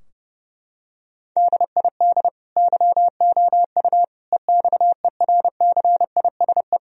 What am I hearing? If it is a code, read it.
Morse code, heard as a keyed tone, about 32 wpm, 700 Hz. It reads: DID YOU EXERCISE
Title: Sentences from Top 1000 Words - Review Part 4 - 30wpm